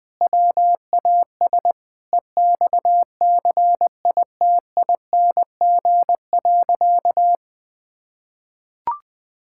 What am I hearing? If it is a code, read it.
Morse code, heard as a keyed tone, about 21 wpm, 700 Hz. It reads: WAS EXCITING.
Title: Sentences with 4-Word Phrases - 20wpm